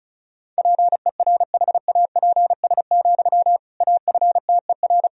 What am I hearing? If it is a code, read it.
Morse code, heard as a keyed tone, about 34 wpm, 700 Hz. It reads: PERHAPS, AFTER
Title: Sentences from Top 500 Words and Abbreviations 35wpm